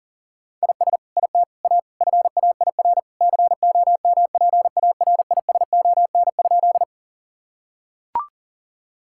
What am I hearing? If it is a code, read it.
Morse code, heard as a keyed tone, about 37 wpm, 700 Hz. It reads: IS IT A FAIR COMPARISON?
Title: Sentences from Top 1600 Words 40wpm